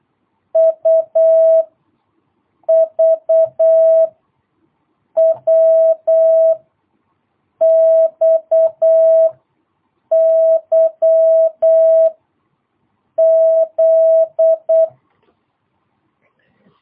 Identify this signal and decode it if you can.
Morse code, heard as a keyed tone, about 8 wpm, spaced as for 5 wpm, 650 Hz. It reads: UVWXYZ
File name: jh8cbh / 2022cw